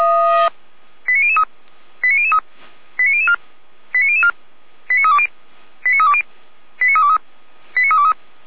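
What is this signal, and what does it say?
Five-tone selective call (ZVEI), heard as quick runs of five tones, tones of about 70 ms. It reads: 89003 89003 89004 89004 89320 89320 89323 89323
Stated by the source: Received on SDR, ZVEI five-tone call, address 89010, from the audio server of Feuerwehr Orlamünde.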